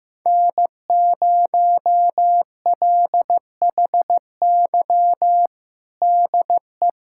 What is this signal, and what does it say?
Morse code, heard as a keyed tone, about 15 wpm, 700 Hz. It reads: N0LHY DE
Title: USA Callsign DE Callsign - 5 Characters - 15wpm